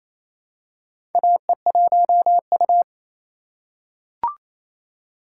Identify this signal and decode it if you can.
Morse code, heard as a keyed tone, about 28 wpm, 700 Hz. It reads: AE1U
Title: USA Call Signs - 4 Characters - 2x Character Spacing - 28wpm